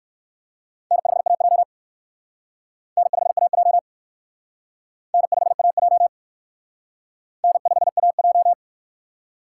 Morse code, heard as a keyed tone, about 60 wpm, 700 Hz. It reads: N5AJ N5AJ N5AJ N5AJ